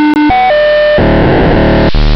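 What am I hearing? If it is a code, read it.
POCSAG pager data.
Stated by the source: POCSAG pager data on 929.060 MHz.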